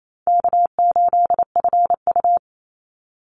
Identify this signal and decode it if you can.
Morse code, heard as a keyed tone, about 28 wpm, 700 Hz. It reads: K8FU
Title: Easy Call Signs Repeated 2x - 28wpm